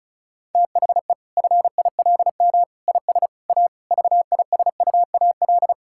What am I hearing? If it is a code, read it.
Morse code, heard as a keyed tone, about 36 wpm, 700 Hz. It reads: THE FILM IS A VISUAL
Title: Sentences from Top 1000 Words - Review Part 6 - 35wpm